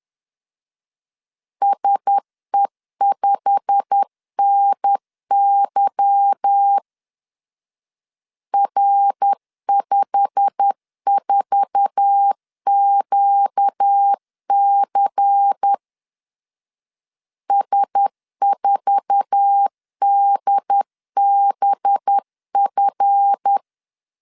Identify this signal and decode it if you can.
Morse code, heard as a keyed tone, about 10 wpm, 800 Hz. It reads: SE5NY R54QC S4DBF
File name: LEZIONI / 8 YJ45 / 8 YJ45 10wpm 2549